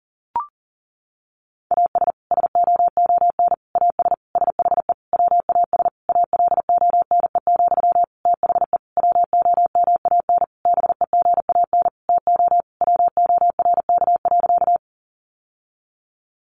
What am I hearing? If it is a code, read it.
Morse code, heard as a keyed tone, about 37 wpm, 700 Hz. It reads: AS SOON AS SHE WAS ALONE, THE WOMAN BEGAN TO WORK.
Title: Sentences from Top 1100 Words 40wpm